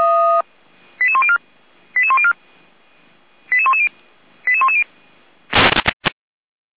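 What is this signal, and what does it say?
Five-tone selective call (ZVEI), heard as quick runs of five tones, tones of about 80 ms. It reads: 88104 88104 88110 88110
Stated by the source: ZVEI five-tone call received by SDR.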